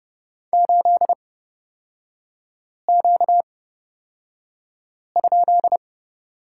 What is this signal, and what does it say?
Morse code, heard as a keyed tone, about 29 wpm, spaced as for 5 wpm, 690 Hz. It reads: 8Q?
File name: course / 30:10wpm / 236 Lesson 38e - Letter q - Three Characters 30wpm